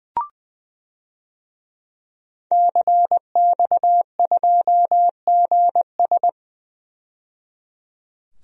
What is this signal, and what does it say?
Morse code, heard as a keyed tone, about 20 wpm, 700 Hz. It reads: CX2GS
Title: DXCC-100 Call Signs - 5 Characters - 20wpm